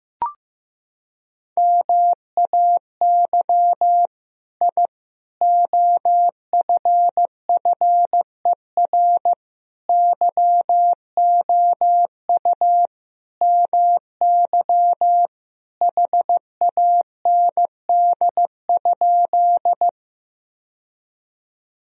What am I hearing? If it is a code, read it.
Morse code, heard as a keyed tone, about 16 wpm, 700 Hz. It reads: MAY I OFFER YOU MY HAND?
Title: Sentences from Top 1000 Words - Review Part 4 - 15wpm